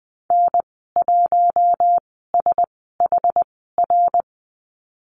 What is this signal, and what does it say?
Morse code, heard as a keyed tone, about 20 wpm, spaced as for 14 wpm, 700 Hz. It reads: N1SHR
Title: 062 Lesson 11g - Letter h - Call Signs 20wpm